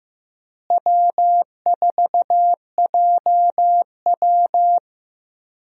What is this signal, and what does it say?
Morse code, heard as a keyed tone, about 15 wpm, 700 Hz. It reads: W4JW
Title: USA Call Signs - 4 Characters - Repeated 3x - 15wpm